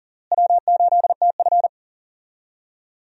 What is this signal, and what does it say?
Morse code, heard as a keyed tone, about 40 wpm, 700 Hz. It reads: W8TF
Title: USA Call Signs - 4 Characters - Repeated 3x - 40wpm